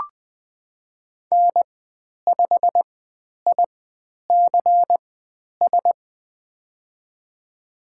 Morse code, heard as a keyed tone, about 20 wpm, spaced as for 10 wpm, 700 Hz. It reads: N5ICS
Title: 102 Lesson 17g - Letter u - Call Signs 20wpm